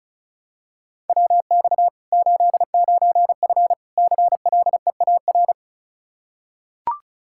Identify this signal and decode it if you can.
Morse code, heard as a keyed tone, about 36 wpm, 700 Hz. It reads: WX 89F CLEAR
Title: Practice QSO - Exchange 2 - Info - 35wpm